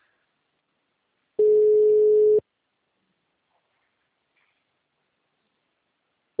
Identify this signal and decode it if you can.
Ringback tone (Europe, 425 Hz).
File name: out- -2007- - - .60075